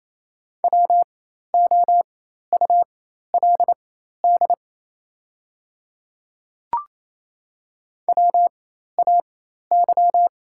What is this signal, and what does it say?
Morse code, heard as a keyed tone, about 28 wpm, spaced as for 13 wpm, 700 Hz. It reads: WOULD WAY
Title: 4x Character Spacing - Top 100 Words - 28wpm